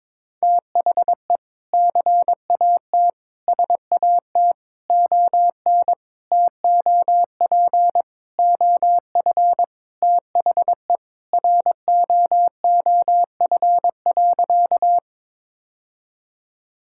Morse code, heard as a keyed tone, about 22 wpm, 700 Hz. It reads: THE CAT SAT ON TOP OF THE ROOF.
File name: practice / 22wpm / Sentences with 3-Word Phrases - 22wpm